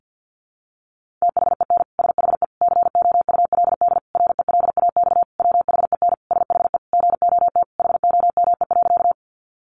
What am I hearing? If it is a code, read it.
Morse code, heard as a keyed tone, about 50 wpm, 700 Hz. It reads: THEN SHE COULD RELAX WHEN SHE GOT HOME.
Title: Sentences from Top 1400 Words 50wpm